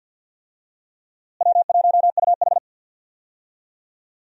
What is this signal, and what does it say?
Morse code, heard as a keyed tone, about 52 wpm, 700 Hz. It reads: W1UH
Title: USA Call Signs - 4 Characters - Repeated 2x - 50wpm